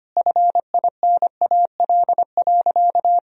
Morse code, heard as a keyed tone, about 25 wpm, 700 Hz. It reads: FINAL.
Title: Sentences from Top 1000 Words - Review Part 2 - 25wpm